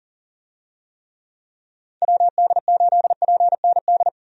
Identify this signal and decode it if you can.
Morse code, heard as a keyed tone, about 38 wpm, 700 Hz. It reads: WD8PND
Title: USA Call Signs - 6 Characters - 40wpm